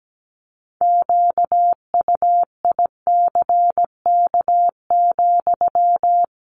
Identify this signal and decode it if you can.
Morse code, heard as a keyed tone, about 17 wpm, 700 Hz. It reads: QUICK,
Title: Sentences from Top 1500 Words 17wpm